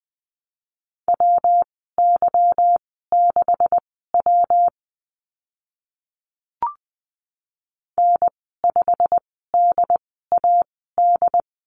Morse code, heard as a keyed tone, about 20 wpm, spaced as for 14 wpm, 700 Hz. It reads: WY6W N5DAD
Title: 161 Lesson 26g - Letter g - Call Signs 20wpm